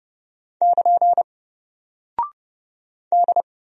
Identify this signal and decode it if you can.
Morse code, heard as a keyed tone, about 31 wpm, 700 Hz. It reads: <KN> D
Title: Single Character (Letter:Number:Punctuation:Prosign) 30wpm